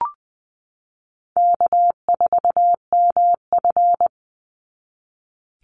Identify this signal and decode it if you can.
Morse code, heard as a keyed tone, about 20 wpm, 700 Hz. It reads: K4MF
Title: NA SOTA Chaser Callsigns 20wpm